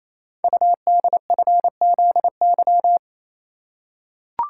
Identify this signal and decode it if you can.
Morse code, heard as a keyed tone, about 29 wpm, 700 Hz. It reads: UDFZY
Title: Code Groups - 5 Letters - 28wpm